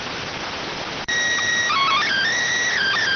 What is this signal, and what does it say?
SSTV (slow-scan television), PD 290.